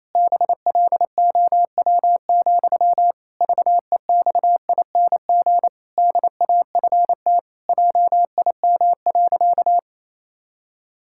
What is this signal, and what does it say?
Morse code, heard as a keyed tone, about 28 wpm, 700 Hz. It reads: BLOW, VEXING DAFT JIM.
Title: Pangram Challenge v2 - 28wpm